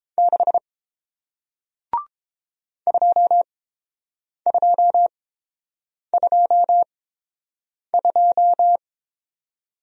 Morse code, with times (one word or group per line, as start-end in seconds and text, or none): 0.2-0.6 6
2.9-8.8 2222